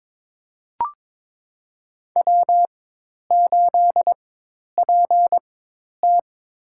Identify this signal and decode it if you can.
Morse code, heard as a keyed tone, about 22 wpm, spaced as for 10 wpm, 700 Hz. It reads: W8PT
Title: USA Call Signs - 4x Character Spacing - 22wpm